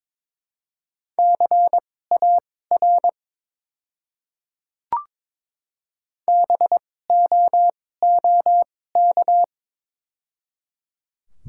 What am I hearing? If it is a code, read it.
Morse code, heard as a keyed tone, about 22 wpm, spaced as for 16 wpm, 700 Hz. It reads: CAR BOOK
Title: 2x Character Spacing - Top 500 Words - 22wpm